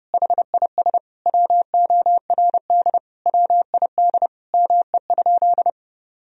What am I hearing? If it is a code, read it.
Morse code, heard as a keyed tone, about 30 wpm, 700 Hz. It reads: HIS WORD WID ME?
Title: Sentences from Top 300 Words and Abbreviations 30wpm